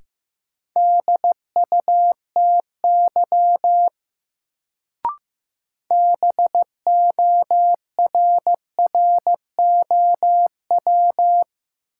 Morse code, heard as a keyed tone, about 15 wpm, 700 Hz. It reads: DUTY BORROW
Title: Sentences from Top 1500 Words - New Word Drill - 15wpm